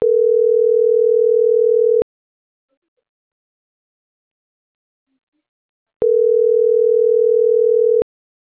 Ringback tone (North America), 440 + 480 Hz.